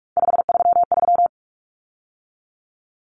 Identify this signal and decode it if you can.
Morse code, heard as a keyed tone, about 44 wpm, 700 Hz. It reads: 533
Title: Numbers - 3-Digit 45wpm